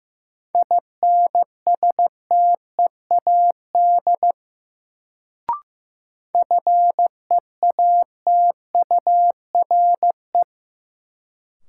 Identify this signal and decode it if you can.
Morse code, heard as a keyed tone, about 15 wpm, 700 Hz. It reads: INSTEAD FEATURE
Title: Sentences from Top 800 Words - New Word Drill - 15wpm